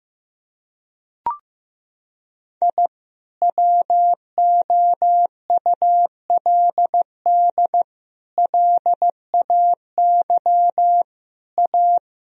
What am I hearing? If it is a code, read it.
Morse code, heard as a keyed tone, about 15 wpm, 700 Hz. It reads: I WOULD LAY A
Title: Sentences from Top 1000 Words - Review Part 3 - 15wpm